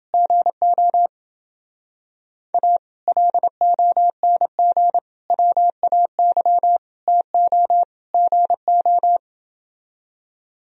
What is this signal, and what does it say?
Morse code, heard as a keyed tone, about 28 wpm, 700 Hz. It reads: GO A LONG WAY TO GO